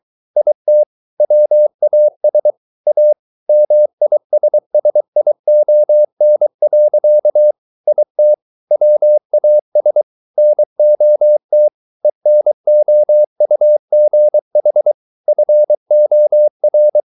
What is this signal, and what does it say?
Morse code, heard as a keyed tone, about 23 wpm, 600 Hz. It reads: IT WAS A MISSION. IT WAS NOT ENOUGH FOR